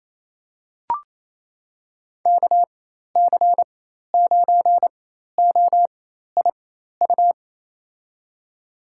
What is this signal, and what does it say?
Morse code, heard as a keyed tone, about 28 wpm, spaced as for 13 wpm, 700 Hz. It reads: KC9OIU
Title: USA Call Signs - 6 Characters - 4x Character Spacing - 28wpm